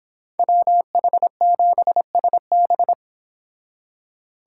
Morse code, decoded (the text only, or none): WH7SB